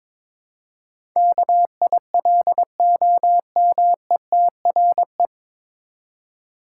Morse code, heard as a keyed tone, about 22 wpm, 700 Hz. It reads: KILOMETRE
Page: Morse Code Ninja practice 22wpm